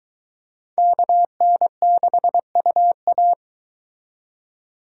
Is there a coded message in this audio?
Yes: Morse code from 0.8 to 3.3 s, heard as a keyed tone, about 23 wpm, 700 Hz. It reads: KN6UA